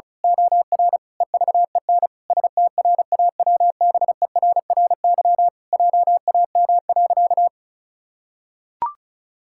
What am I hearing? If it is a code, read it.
Morse code, heard as a keyed tone, about 34 wpm, 700 Hz. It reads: OR EVEN STRAWBERRY JAM.
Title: Pangram Challenge v2 - 35wpm